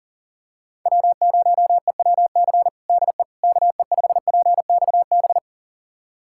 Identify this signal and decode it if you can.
Morse code, heard as a keyed tone, about 40 wpm, 700 Hz. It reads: W0EWC DE KE5PXB
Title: USA Callsign DE Callsign 40wpm